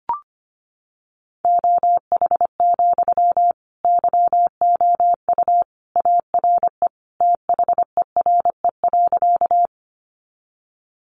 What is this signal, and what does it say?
Morse code, heard as a keyed tone, about 25 wpm, 700 Hz. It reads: OH, YOU ARE THERE.
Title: Sentences from Top 500 Words 25wpm